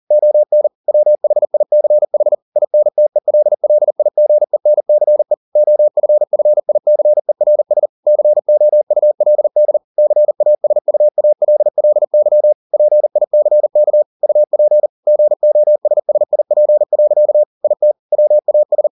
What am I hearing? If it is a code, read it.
Morse code, heard as a keyed tone, about 40 wpm, 600 Hz. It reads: ON WHICH INTELLIGENCE OFFICERS COULD CASUALLY PICK UP GOSSIP. IT WAS